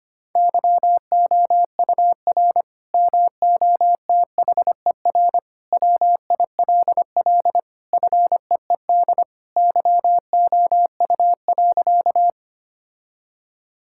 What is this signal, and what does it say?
Morse code, heard as a keyed tone, about 25 wpm, 700 Hz. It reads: YOUR MOTHER WILL FEED YOU.
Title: Sentences from Top 1600 Words 25wpm